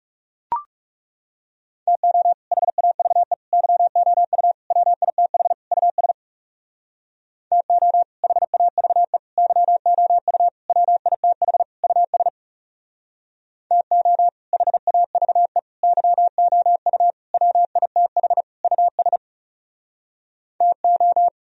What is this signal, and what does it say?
Morse code, heard as a keyed tone, about 38 wpm, 700 Hz. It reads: TO HAVE YOU WITH US TO HAVE YOU WITH US TO HAVE YOU WITH US TO